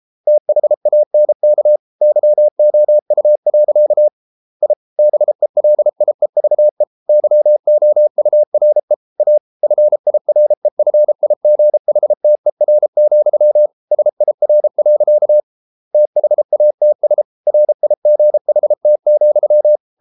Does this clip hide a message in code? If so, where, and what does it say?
Yes: Morse code from 0.3 to 19.8 s, heard as a keyed tone, about 33 wpm, 600 Hz. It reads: THANK YOU. I BELIEVE YOURE A FIREFIGHTER, SIR. THATS RIGHT,